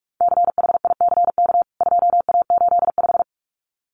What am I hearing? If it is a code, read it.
Morse code, heard as a keyed tone, about 44 wpm, 700 Hz. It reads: CHICK 2A85
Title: CWT Exchange 45wpm